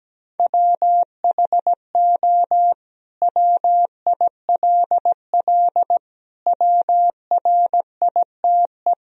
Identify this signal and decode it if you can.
Morse code, heard as a keyed tone, about 17 wpm, 700 Hz. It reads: WHO WILL WRITE
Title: Sentences from Top 1000 Words - Review Part 3 - 17wpm